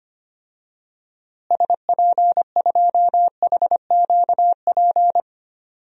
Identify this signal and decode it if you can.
Morse code, heard as a keyed tone, about 25 wpm, 700 Hz. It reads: SP2HQP
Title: DXCC-100 Call Signs - 6 Characters - Repeated 3x - 25wpm